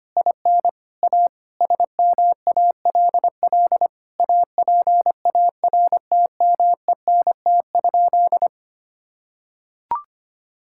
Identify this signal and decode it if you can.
Morse code, heard as a keyed tone, about 25 wpm, 700 Hz. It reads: IN A SMALL APARTMENT?
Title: Sentences from Top 1000 Words - Review Part 6 - 25wpm